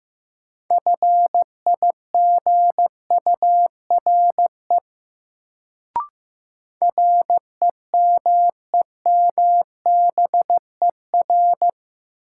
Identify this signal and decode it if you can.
Morse code, heard as a keyed tone, about 15 wpm, 700 Hz. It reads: FIGURE REMEMBER
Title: Sets of 4 Words - New Word Drill - 15wpm